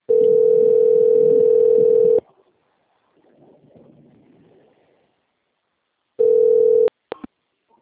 Ringback tone (North America), 440 + 480 Hz.